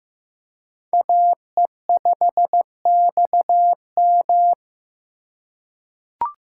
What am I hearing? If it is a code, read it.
Morse code, heard as a keyed tone, about 15 wpm, 700 Hz. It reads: AE5XM
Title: USA Call Signs - 5 Characters - 4x Character Spacing - 15wpm